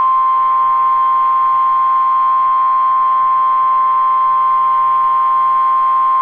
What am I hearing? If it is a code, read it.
Emergency Alert System (SAME) tones (the 1050 Hz attention tone). It is heard from 0.8 to 5.8 s.